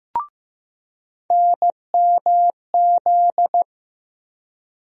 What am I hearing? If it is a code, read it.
Morse code, heard as a keyed tone, about 15 wpm, 700 Hz. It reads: NMZ
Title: USA Call Sign Postfixes - 3 Letters 15wpm